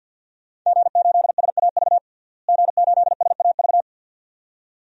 Morse code, heard as a keyed tone, about 49 wpm, 700 Hz. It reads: G8SAV G8SAV
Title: DXCC-100 Call Signs - 5 Characters - Repeated 2x - 50wpm